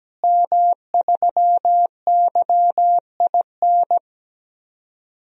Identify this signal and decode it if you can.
Morse code, heard as a keyed tone, about 17 wpm, 700 Hz. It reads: M3YIN
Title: DXCC-100 Call Signs - 5 Characters - Repeated 2x - 17wpm